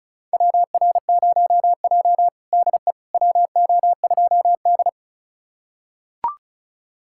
Morse code, heard as a keyed tone, about 36 wpm, 700 Hz. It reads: WR0J DE WO2D
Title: USA Callsign DE Callsign - 4 Characters - 35wpm